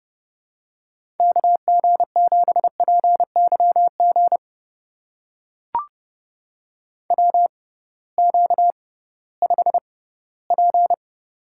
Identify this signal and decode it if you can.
Morse code, heard as a keyed tone, about 30 wpm, spaced as for 18 wpm, 700 Hz. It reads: KG7PYG WQ5P
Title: USA Call Signs - 6x Character Spacing - 30wpm